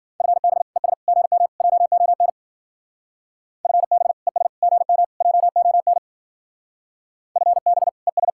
Morse code, heard as a keyed tone, about 49 wpm, 700 Hz. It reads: FB ES GM JON FB ES GM JON FB ES
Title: Speed-Racing Common QSO Phrases Encore - 40wpm